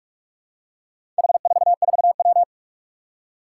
Morse code, heard as a keyed tone, about 44 wpm, 700 Hz. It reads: H44W
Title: Four Challenging Letter-Number Combinations 45wpm